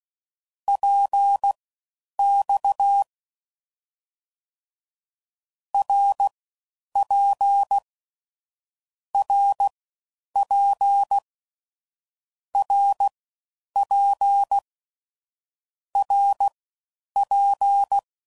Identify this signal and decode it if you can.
Morse code, heard as a keyed tone, about 16 wpm, spaced as for 9 wpm, 800 Hz. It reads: PX RP RP RP RP